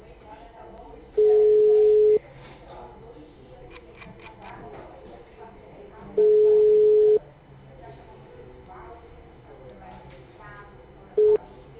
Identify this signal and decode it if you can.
Ringback tone (Europe, 425 Hz), with speech.